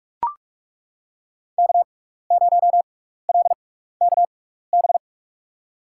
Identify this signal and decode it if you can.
Morse code, heard as a keyed tone, about 43 wpm, spaced as for 15 wpm, 700 Hz. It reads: K0LKB
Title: USA Call Signs - 5 Characters - 6x Character Spacing - 45wpm